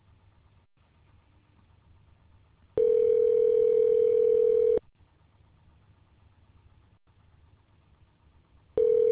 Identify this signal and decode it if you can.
Ringback tone (North America), 440 + 480 Hz.